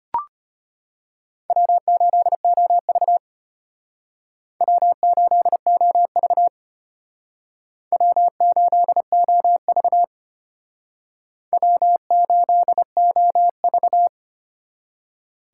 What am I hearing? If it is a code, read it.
Morse code, heard as a keyed tone, about 32 wpm, 700 Hz. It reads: W8OV W8OV W8OV W8OV